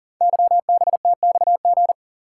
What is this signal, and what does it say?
Morse code, heard as a keyed tone, about 38 wpm, 690 Hz. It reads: YBTXG
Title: Code Groups - 5 Letters - 40wpm